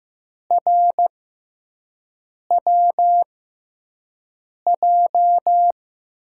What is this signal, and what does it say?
Morse code, heard as a keyed tone, about 15 wpm, spaced as for 5 wpm, 700 Hz. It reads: RWJ
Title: USA Call Signs - 6 Characters - 6x Character Spacing - 15wpm